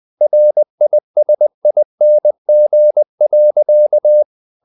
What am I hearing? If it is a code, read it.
Morse code, heard as a keyed tone, about 20 wpm, 600 Hz. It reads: RISING.
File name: the old man and the sea / the old man and the sea part 1 20wpm